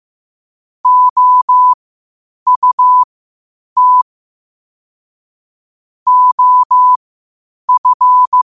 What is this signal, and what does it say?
Morse code, heard as a keyed tone, about 15 wpm, spaced as for 8 wpm, 1000 Hz. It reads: OUT OF
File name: qotdfiles / 8wpm